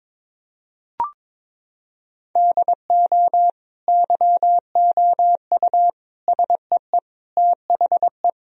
Morse code, heard as a keyed tone, about 22 wpm, 700 Hz. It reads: DO YOU SEE THE
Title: Sentences from Top 1000 Words - Review Part 4 - 22wpm